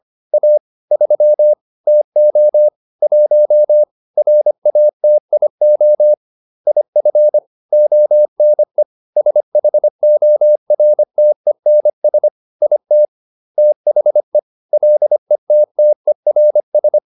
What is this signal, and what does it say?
Morse code, heard as a keyed tone, about 25 wpm, 600 Hz. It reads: A 3 TO 1 RATIO IF ONE SHORTENS IT THE LETTERS